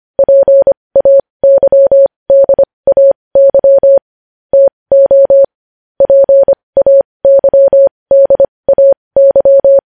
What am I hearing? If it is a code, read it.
Morse code, heard as a keyed tone, about 25 wpm, spaced as for 20 wpm, 550 Hz. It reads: PAYDAY TO PAYDAY
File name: podcast / News headlines-1x-25x20